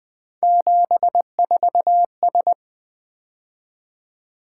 Morse code, heard as a keyed tone, about 20 wpm, 700 Hz. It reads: 74S
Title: Three Letter-Number Combinations - 20wpm